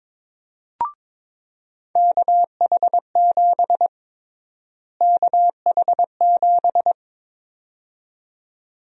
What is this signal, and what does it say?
Morse code, heard as a keyed tone, about 22 wpm, 700 Hz. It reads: KH7 KH7
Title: USA Call Sign Prefixes - Repeated 2x - 22wpm